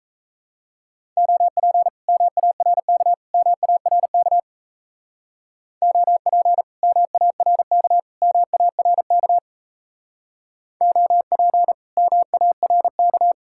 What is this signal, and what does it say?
Morse code, heard as a keyed tone, about 35 wpm, 700 Hz. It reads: OP MARK MARK OP MARK MARK OP MARK
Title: Speed-Racing Common QSO Phrases Encore - 28wpm